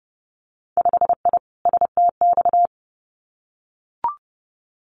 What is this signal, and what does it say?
Morse code, heard as a keyed tone, about 30 wpm, 700 Hz. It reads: 5I STX
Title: Winter Field Day Exchanges Encore 30wpm